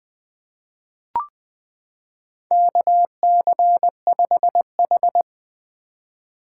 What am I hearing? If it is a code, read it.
Morse code, heard as a keyed tone, about 20 wpm, 700 Hz. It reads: KC5H
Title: 2023 POTA Activator Call Signs - 20wpm